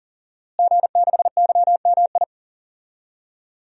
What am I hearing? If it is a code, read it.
Morse code, heard as a keyed tone, about 40 wpm, 700 Hz. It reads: G6YMI
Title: DXCC-100 Call Signs - 5 Characters - Repeated 2x - 40wpm